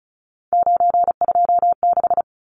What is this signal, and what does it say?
Morse code, heard as a keyed tone, about 34 wpm, 700 Hz. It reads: 926
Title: Numbers - 3-Digit 35wpm